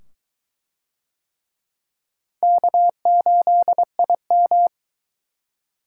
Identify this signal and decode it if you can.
Morse code, heard as a keyed tone, about 22 wpm, 700 Hz. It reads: K8IM